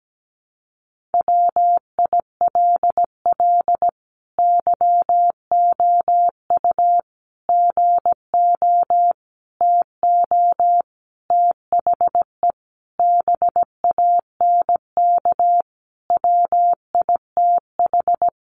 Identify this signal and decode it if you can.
Morse code, heard as a keyed tone, about 17 wpm, 700 Hz. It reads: WILL YOU GO TO THE BANK WITH
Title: Sentences from Top 900 Words 17wpm